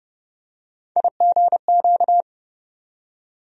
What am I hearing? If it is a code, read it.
Morse code, heard as a keyed tone, about 30 wpm, 700 Hz. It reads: IGQ
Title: USA Call Sign Postfixes - 3 Letters 30wpm